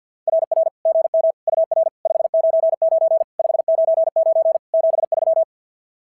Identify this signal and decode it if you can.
Morse code, heard as a keyed tone, about 49 wpm, 650 Hz. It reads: RR GM UR 599 599 73
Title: SOTA QSO Practice 50wpm